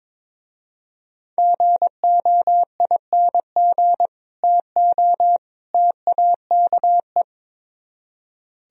Morse code, heard as a keyed tone, about 24 wpm, 700 Hz. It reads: GOING TO TAKE
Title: Speed-Racing - Sets of 3 Words 22wpm